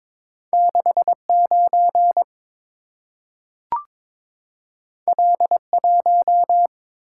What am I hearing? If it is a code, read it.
Morse code, heard as a keyed tone, about 23 wpm, 690 Hz. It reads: 69 L1
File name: practice / 22wpm / Two Letter-Number Combinations - 22wpm